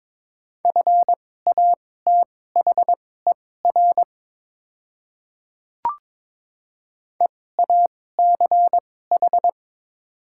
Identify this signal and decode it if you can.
Morse code, heard as a keyed tone, about 22 wpm, spaced as for 16 wpm, 700 Hz. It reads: FATHER EACH
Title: 2x Character Spacing - Top 200 Words - 22wpm